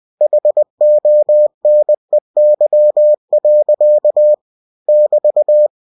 Morse code, heard as a keyed tone, about 20 wpm, 600 Hz. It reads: HONEY. <BT>
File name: cw-books / winnie-the-pooh / winnie-20wpm-page-0007